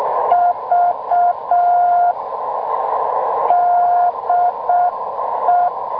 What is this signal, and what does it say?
Morse code, heard as a keyed tone, about 6 wpm, 700 Hz, with radio noise. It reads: V DE